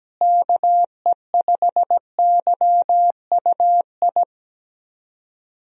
Morse code, heard as a keyed tone, about 17 wpm, 700 Hz. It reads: KE5YUI
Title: USA Call Signs - 6 Characters - Repeated 2x - 17wpm